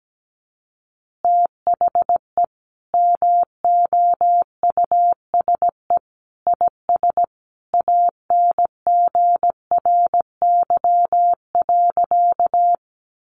Morse code, heard as a keyed tone, about 17 wpm, 700 Hz. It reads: THE MOUSE IS ANGRY.
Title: Easy Phrases 17wpm